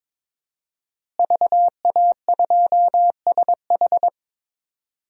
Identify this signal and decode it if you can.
Morse code, heard as a keyed tone, about 22 wpm, 700 Hz. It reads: VA2SH
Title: DXCC-100 Call Signs - 5 Characters - Repeated 3x - 22wpm